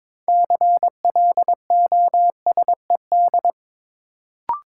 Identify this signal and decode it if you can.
Morse code, heard as a keyed tone, about 22 wpm, 700 Hz. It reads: CLOSED
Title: Sentences from Top 1400 Words - New Word Drill Alt - 22wpm